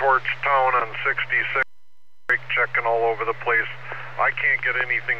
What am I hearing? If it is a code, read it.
Emergency Alert System (SAME) tones, heard as harsh buzzing bursts, with speech. It is heard from 0.9 to 4.0 s.